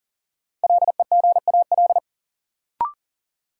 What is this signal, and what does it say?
Morse code, heard as a keyed tone, about 40 wpm, 700 Hz. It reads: LEGAL
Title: Sentences from Top 1200 Words - New Word Drill Alt - 40wpm